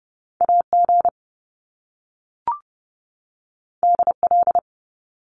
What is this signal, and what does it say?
Morse code, heard as a keyed tone, about 30 wpm, 700 Hz. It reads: AG DL
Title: Top Two Letter Combinations in Words 30wpm